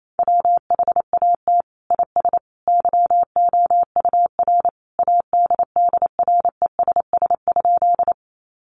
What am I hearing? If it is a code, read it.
Morse code, heard as a keyed tone, about 28 wpm, 700 Hz. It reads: WHAT IS YOUR ADDRESS?
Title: Sentences from Top 1000 Words 28wpm